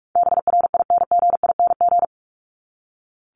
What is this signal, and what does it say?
Morse code, heard as a keyed tone, about 44 wpm, 700 Hz. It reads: BRINGING
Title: ING Suffix - Top 100 Words with Base Words 45wpm